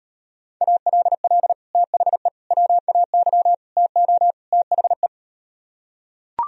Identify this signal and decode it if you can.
Morse code, heard as a keyed tone, about 38 wpm, 700 Hz. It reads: ALL THE WAY TO THE